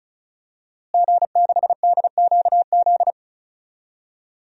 Morse code, heard as a keyed tone, about 34 wpm, 700 Hz. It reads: G6DQZ
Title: DXCC-100 Call Signs - 5 Characters - Repeated 2x - 35wpm